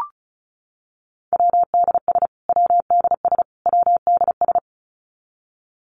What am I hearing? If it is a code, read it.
Morse code, heard as a keyed tone, about 34 wpm, 700 Hz. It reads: WDS WDS WDS
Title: Abbreviations Repeated 3x - 35wpm